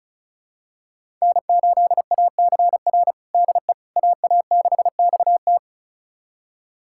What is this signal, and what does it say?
Morse code, heard as a keyed tone, about 36 wpm, 700 Hz. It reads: N8ACR DE AA6XT
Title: USA Callsign DE Callsign 35wpm